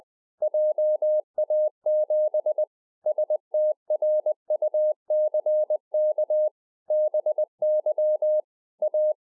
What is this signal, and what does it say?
Morse code, heard as a keyed tone, about 20 wpm, 610 Hz. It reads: JA7 STRUCK BY A